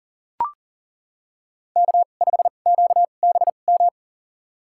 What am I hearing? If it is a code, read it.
Morse code, heard as a keyed tone, about 37 wpm, spaced as for 28 wpm, 700 Hz. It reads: K5QBM
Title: USA Call Signs - 5 Characters - 2x Character Spacing - 40wpm